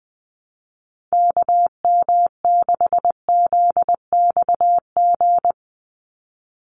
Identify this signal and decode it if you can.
Morse code, heard as a keyed tone, about 20 wpm, 700 Hz. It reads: KM6ZXG
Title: USA Call Signs 20wpm